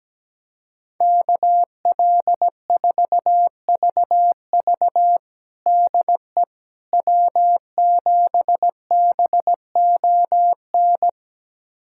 Morse code, heard as a keyed tone, about 17 wpm, 700 Hz. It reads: KL4VV DE W7BON
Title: USA Callsign DE Callsign - 5 Characters - 17wpm